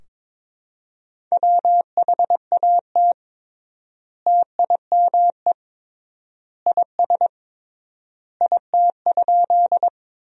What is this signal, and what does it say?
Morse code, heard as a keyed tone, about 21 wpm, 700 Hz. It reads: WHAT TIME IS IT?
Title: 3x Word-Spacing - Sentences from Top 100 Words - 22wpm